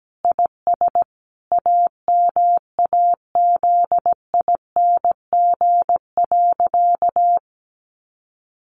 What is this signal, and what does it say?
Morse code, heard as a keyed tone, about 17 wpm, 700 Hz. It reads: IS AMAZING.
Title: Sentences from Top 1500 Words 17wpm